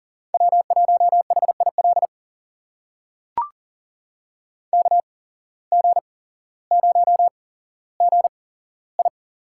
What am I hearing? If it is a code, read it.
Morse code, heard as a keyed tone, about 38 wpm, spaced as for 17 wpm, 700 Hz. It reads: W1HIL KG0GI
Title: USA Call Signs - 5 Characters - 8x Character Spacing - 40wpm